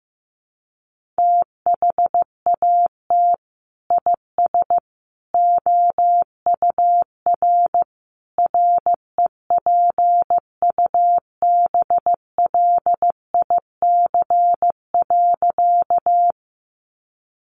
Morse code, heard as a keyed tone, about 15 wpm, 700 Hz. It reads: THAT IS OUR REPUBLIC.
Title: Sentences from Top 1500 Words 15wpm